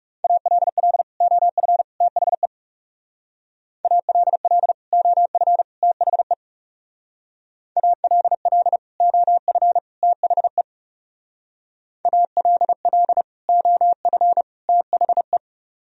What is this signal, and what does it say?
Morse code, heard as a keyed tone, about 37 wpm, 700 Hz. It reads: ALL OF THE ALL OF THE ALL OF THE ALL OF THE
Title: Speed-Racing - Sets of 3 Words 30wpm